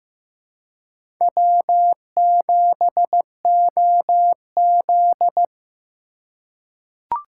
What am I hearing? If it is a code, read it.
Morse code, heard as a keyed tone, about 15 wpm, 700 Hz. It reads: W7OZ
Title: USA Call Signs - 4 Characters - 3x Character Spacing - 15wpm